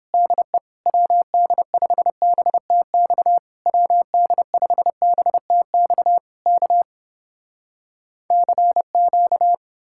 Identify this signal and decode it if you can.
Morse code, heard as a keyed tone, about 30 wpm, 700 Hz. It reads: DE WD5BTX WD5BTX K CQ